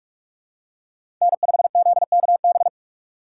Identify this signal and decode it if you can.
Morse code, heard as a keyed tone, about 45 wpm, 700 Hz. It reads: N5ZKB
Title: USA Call Signs - 2x Character Spacing - 45wpm